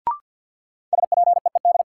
Morse code, heard as a keyed tone, about 52 wpm, 700 Hz. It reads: SPEED
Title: Sentences from Top 800 Words - New Word Drill - 50wpm